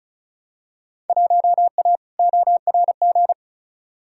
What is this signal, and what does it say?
Morse code, heard as a keyed tone, about 34 wpm, 700 Hz. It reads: 1A ORG
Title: Field Day Exchanges - 35wpm